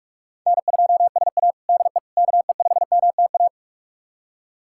Morse code, heard as a keyed tone, about 46 wpm, 700 Hz. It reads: N2SA DE KE5MTA
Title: USA Callsign DE Callsign 45wpm